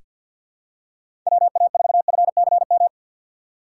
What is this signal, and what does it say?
Morse code, heard as a keyed tone, about 48 wpm, 700 Hz. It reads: WA4FCM
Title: USA Call Signs - 6 Characters - 3x Character Spacing - 50wpm